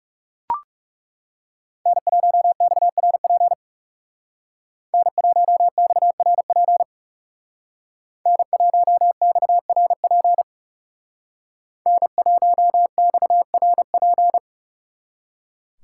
Morse code, heard as a keyed tone, about 38 wpm, 700 Hz. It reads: N1XRP N1XRP N1XRP N1XRP